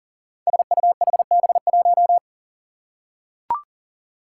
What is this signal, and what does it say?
Morse code, heard as a keyed tone, about 39 wpm, 700 Hz. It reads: SUHB1
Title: Five Challenging Letter-Number Combinations 40wpm